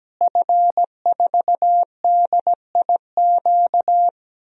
Morse code, heard as a keyed tone, about 17 wpm, 700 Hz. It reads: F4DIQ
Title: DXCC-100 Call Signs - 5 Characters - Repeated 3x - 17wpm